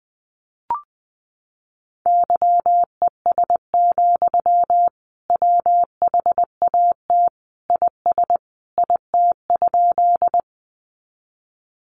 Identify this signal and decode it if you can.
Morse code, heard as a keyed tone, about 19 wpm, 700 Hz. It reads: YES, WHAT IS IT?
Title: Sentences from Top 500 Words 20wpm